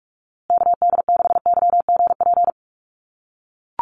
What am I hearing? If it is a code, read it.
Morse code, heard as a keyed tone, about 44 wpm, 700 Hz. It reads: KD6YGP
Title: USA Call Signs Repeated 3x 45wpm